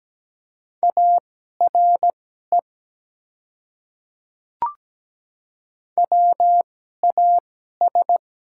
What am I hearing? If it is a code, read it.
Morse code, heard as a keyed tone, about 17 wpm, spaced as for 12 wpm, 700 Hz. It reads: ARE WAS
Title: 2x Character Spacing - Top 100 Words - 17wpm